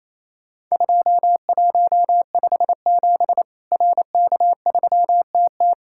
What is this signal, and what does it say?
Morse code, heard as a keyed tone, about 28 wpm, 700 Hz. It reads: 2157 RK3TT
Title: Worked All Europe DX Contest - QTC Exchange - 28wpm